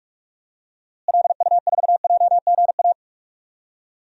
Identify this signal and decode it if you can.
Morse code, heard as a keyed tone, about 44 wpm, 700 Hz. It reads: LU4JGA
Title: DXCC-100 Call Signs - 6 Characters - Repeated 3x - 45wpm